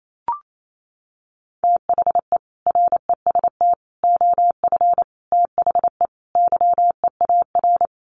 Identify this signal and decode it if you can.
Morse code, heard as a keyed tone, about 28 wpm, 700 Hz. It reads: THE REST OF THE YEAR
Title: Sets of 5 Words Encore 28wpm